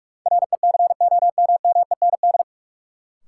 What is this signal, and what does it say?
Morse code, heard as a keyed tone, about 46 wpm, 700 Hz. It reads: RECOMMEND
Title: Sentences from Top 1200 Words - New Word Drill Alt - 45wpm